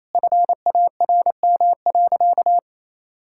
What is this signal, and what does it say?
Morse code, heard as a keyed tone, about 27 wpm, 700 Hz. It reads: FARM.
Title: Sentences from Top 1000 Words - Review Part 5 - 28wpm